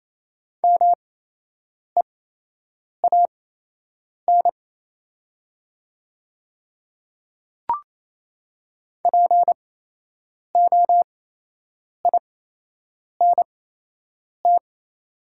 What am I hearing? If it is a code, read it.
Morse code, heard as a keyed tone, about 28 wpm, spaced as for 8 wpm, 700 Hz. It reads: MEAN POINT
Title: 8x Character Spacing - Top 200 Words - 28wpm